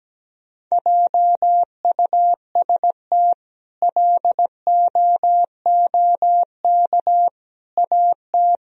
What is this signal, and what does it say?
Morse code, heard as a keyed tone, about 17 wpm, 700 Hz. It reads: JUST LOOK AT
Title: Sentences from Top 1000 Words - Review Part 1 - 17wpm